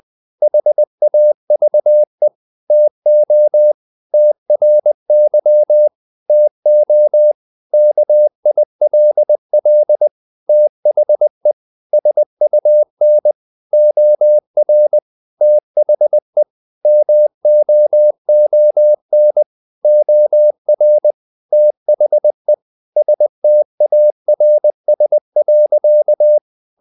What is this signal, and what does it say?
Morse code, heard as a keyed tone, about 20 wpm, 600 Hz. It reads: HAVE TO TRY TO KILL THE SUN OR THE MOON OR THE STARS.
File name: the old man and the sea part 2 20wpm